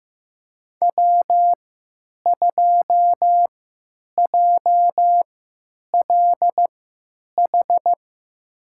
Morse code, heard as a keyed tone, about 15 wpm, spaced as for 9 wpm, 700 Hz. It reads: W2JLH
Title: USA Call Signs - 5 Characters - 3x Character Spacing - 15wpm